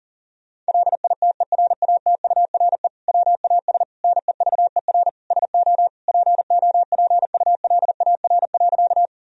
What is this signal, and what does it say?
Morse code, heard as a keyed tone, about 40 wpm, 700 Hz. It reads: LITERATURE WAS NEVER SO POPULAR.
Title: Sentences from Top 1000 Words - Review Part 3 - 40wpm